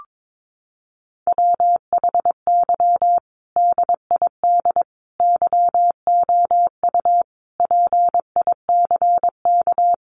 Morse code, heard as a keyed tone, about 22 wpm, 700 Hz. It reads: WHY DID YOU PICK
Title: Sentences from Top 1000 Words 22wpm